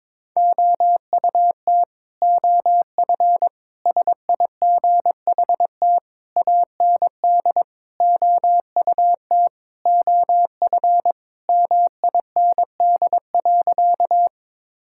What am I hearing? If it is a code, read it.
Morse code, heard as a keyed tone, about 22 wpm, 700 Hz. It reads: OUT OF SIGHT AND OUT OF MIND.
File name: Sentences with 7-Word Phrases - 22wpm